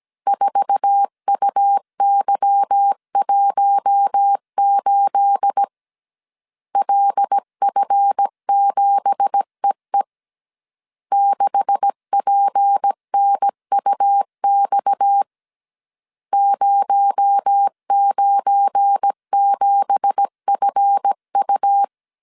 Morse code, heard as a keyed tone, about 17 wpm, 800 Hz. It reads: 4UY18 LF7EE 6PNUX 097FU
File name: CW / DET / 1 9 1500C S A 16wpm 2828